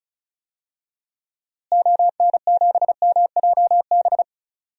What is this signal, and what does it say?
Morse code, heard as a keyed tone, about 34 wpm, 700 Hz. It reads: ON7MJB
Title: DXCC-100 Call Signs - 6 Characters - Repeated 3x - 35wpm